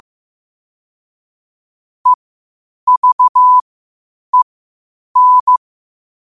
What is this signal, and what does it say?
Morse code, heard as a keyed tone, about 15 wpm, spaced as for 8 wpm, 1000 Hz. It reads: EVEN
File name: qotdfiles / 8wpm